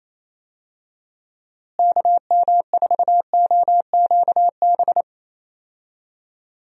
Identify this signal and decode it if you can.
Morse code, heard as a keyed tone, about 28 wpm, 700 Hz. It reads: KM4OQB
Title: USA Call Signs - 6 Characters - Repeated 2x - 28wpm